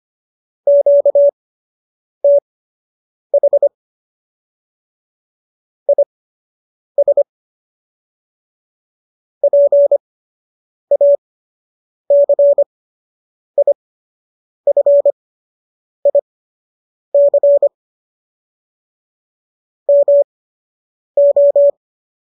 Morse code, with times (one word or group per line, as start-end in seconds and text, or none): 0.7-3.7 QTH
5.9-7.2 IS
9.4-17.7 PACIFIC
19.9-21.7 MO